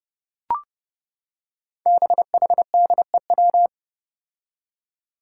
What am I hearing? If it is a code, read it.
Morse code, heard as a keyed tone, about 30 wpm, 700 Hz. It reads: BHDEW
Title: Five Challenging Letter-Number Combinations 30wpm